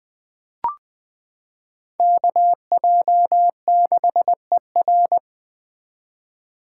Morse code, heard as a keyed tone, about 20 wpm, 700 Hz. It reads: KJ6ER
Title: 2023 POTA Activator Call Signs - 20wpm